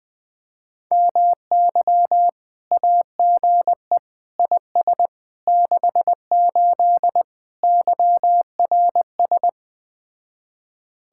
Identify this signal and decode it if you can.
Morse code, heard as a keyed tone, about 20 wpm, 700 Hz. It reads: MY AGE IS 68 YRS
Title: Speed-Racing Common QSO Phrases Encore - 17wpm